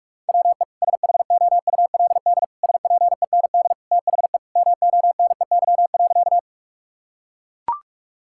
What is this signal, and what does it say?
Morse code, heard as a keyed tone, about 41 wpm, 700 Hz. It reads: WE SHOULD SPEND THE MONEY.
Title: Sentences from Top 1000 Words - Review Part 5 - 45wpm